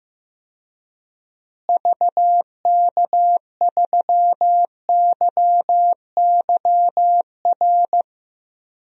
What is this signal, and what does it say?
Morse code, heard as a keyed tone, about 15 wpm, 700 Hz. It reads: VK3YYR
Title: DXCC-100 Call Signs - 6 Characters - Repeated 2x - 15wpm